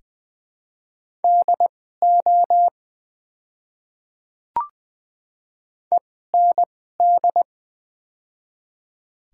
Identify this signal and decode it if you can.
Morse code, heard as a keyed tone, about 20 wpm, spaced as for 14 wpm, 700 Hz. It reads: DO END
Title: 2x Character Spacing - Top 200 Words - 20wpm